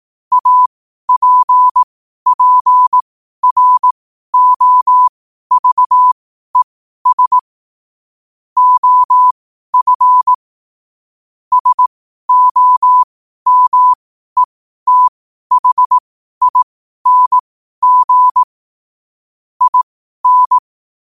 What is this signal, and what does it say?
Morse code, heard as a keyed tone, about 18 wpm, spaced as for 12 wpm, 1000 Hz. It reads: APPROVES OF SOMETHING IN
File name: qotdfiles / 12wpm